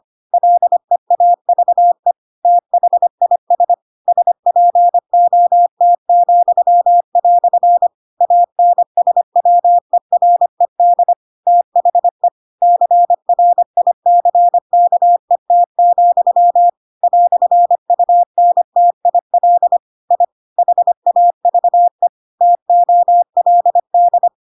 Morse code, heard as a keyed tone, about 25 wpm, 700 Hz. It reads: LEAVE THIS SPOT," ANSWERED THE CRICKET, "UNTIL I HAVE TOLD
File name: Pinocchio / Pinocchio ch4 25 25 ik1hge